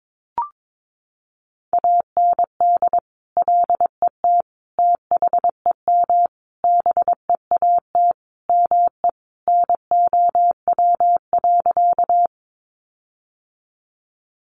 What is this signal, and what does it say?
Morse code, heard as a keyed tone, about 22 wpm, 700 Hz. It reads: AND LET THEM BEAT ME NOW.
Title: Sentences from Top 1300 Words 22wpm